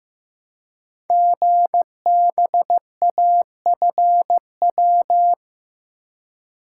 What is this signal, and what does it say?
Morse code, heard as a keyed tone, about 15 wpm, 700 Hz. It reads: GBAFW
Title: Code Groups - 5 Letters - 15wpm